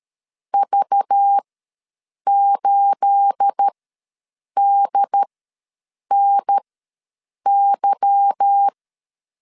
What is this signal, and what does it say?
Morse code, heard as a keyed tone, about 13 wpm, spaced as for 7 wpm, 800 Hz. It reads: V8DNY
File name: LEZIONI / 9 28 ws 12wpm 3354